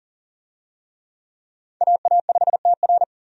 Morse code, heard as a keyed tone, about 40 wpm, 700 Hz. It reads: AA5TR
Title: USA Call Signs - 5 Characters - 40wpm